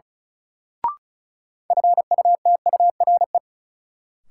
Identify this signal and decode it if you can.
Morse code, heard as a keyed tone, about 34 wpm, 700 Hz. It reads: FUTURE
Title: Sets of 3 Words - New Word Drill - 35wpm